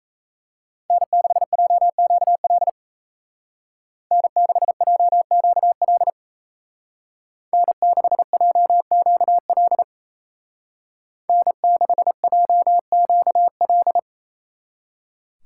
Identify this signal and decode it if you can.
Morse code, heard as a keyed tone, about 34 wpm, 700 Hz. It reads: N6JQL N6JQL N6JQL N6JQL
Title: Speed Racing - USA 1x3 Call Signs 28wpm